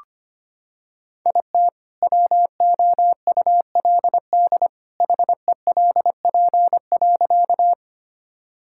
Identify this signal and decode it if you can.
Morse code, heard as a keyed tone, about 25 wpm, 700 Hz. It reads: IT WOULD HELP.
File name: practice / 25wpm / Sentences with 2-Word Phrases - 25wpm